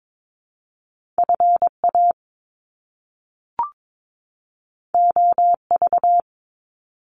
Morse code, heard as a keyed tone, about 22 wpm, 700 Hz. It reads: FA OV
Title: Top Two Letter Combinations in Words 22wpm